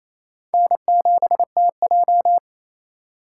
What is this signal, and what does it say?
Morse code, heard as a keyed tone, about 27 wpm, 700 Hz. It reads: N7TJ
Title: USA Call Signs - 4 Characters - 6x Character Spacing - 28wpm